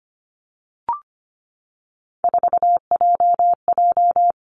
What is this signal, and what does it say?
Morse code, heard as a keyed tone, about 25 wpm, 700 Hz. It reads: 4JJ
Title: Three Challenging Letter-Number Combinations 25wpm